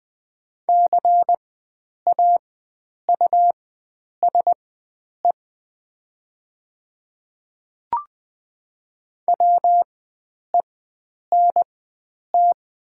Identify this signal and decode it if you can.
Morse code, heard as a keyed tone, about 20 wpm, spaced as for 9 wpm, 700 Hz. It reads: CAUSE WENT
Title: 4x Character Spacing - Top 300 Words - 20wpm